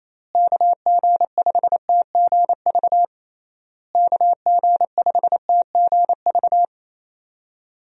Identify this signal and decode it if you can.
Morse code, heard as a keyed tone, about 27 wpm, 700 Hz. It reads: KG5TGV KG5TGV
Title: USA Call Signs - 6 Characters - Repeated 3x - 28wpm